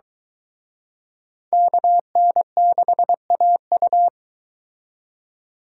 Morse code, heard as a keyed tone, about 23 wpm, 700 Hz. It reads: KN6AU